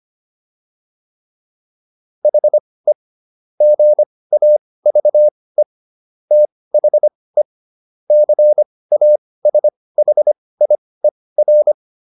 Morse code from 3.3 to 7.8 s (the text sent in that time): GAVE THE